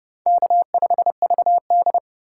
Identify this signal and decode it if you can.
Morse code, heard as a keyed tone, about 32 wpm, 700 Hz. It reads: K5VD
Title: USA Call Signs - 2x Character Spacing - 30wpm